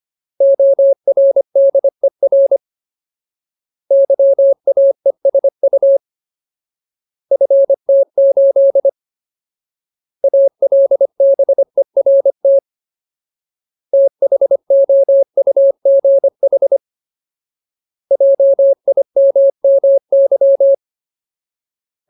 Morse code, heard as a keyed tone, about 25 wpm, 550 Hz. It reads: ORDER YAESU FT8 ALBERT THOUGH JIMMY
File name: Ultimate Word List / 25WPM Ultimate